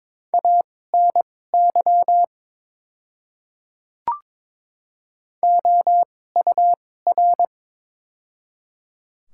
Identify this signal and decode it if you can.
Morse code, heard as a keyed tone, about 22 wpm, spaced as for 16 wpm, 700 Hz. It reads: ANY OUR
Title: 2x Character Spacing - Top 200 Words - 22wpm